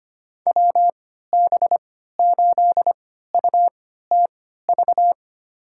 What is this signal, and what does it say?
Morse code, heard as a keyed tone, about 25 wpm, spaced as for 14 wpm, 700 Hz. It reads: WB8UTV
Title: USA Call Signs - 6 Characters - 3x Character Spacing - 25wpm